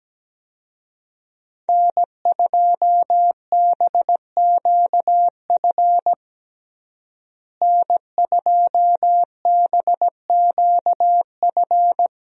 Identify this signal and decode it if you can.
Morse code, heard as a keyed tone, about 17 wpm, 700 Hz. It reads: N2BQF N2BQF
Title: USA Call Signs - 5 Characters - Repeated 3x - 17wpm